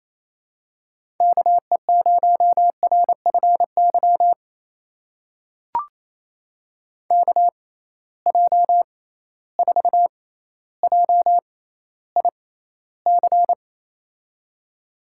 Morse code, heard as a keyed tone, about 27 wpm, spaced as for 14 wpm, 700 Hz. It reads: KE0RFY KJ4JIC
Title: USA Call Signs - 6 Characters - 6x Character Spacing - 28wpm